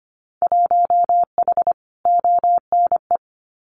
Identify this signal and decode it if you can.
Morse code, heard as a keyed tone, about 25 wpm, 690 Hz. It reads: 1H ONE
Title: Winter Field Day Exchanges Encore 25wpm